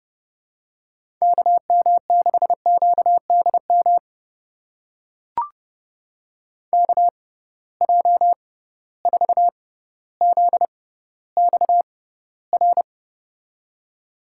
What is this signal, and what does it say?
Morse code, heard as a keyed tone, about 31 wpm, spaced as for 16 wpm, 700 Hz. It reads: KM6QDM KJ4ZXR
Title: USA Call Signs - 6 Characters - 6x Character Spacing - 30wpm